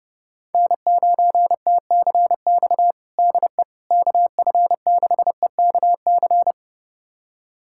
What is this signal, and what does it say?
Morse code, heard as a keyed tone, about 30 wpm, 700 Hz. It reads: N9TCX DE KF6EKC
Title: USA Callsign DE Callsign 30wpm